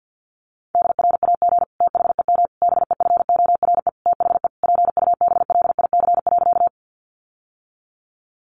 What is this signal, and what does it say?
Morse code, heard as a keyed tone, about 50 wpm, 700 Hz. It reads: DRAG THEM BEFORE THE PUBLIC.
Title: Sentences from Top 1500 Words 50wpm